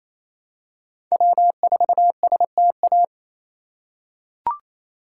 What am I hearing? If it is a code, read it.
Morse code, heard as a keyed tone, about 28 wpm, 700 Hz. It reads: W4STA
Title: USA Call Signs - 5 Characters - 4x Character Spacing - 28wpm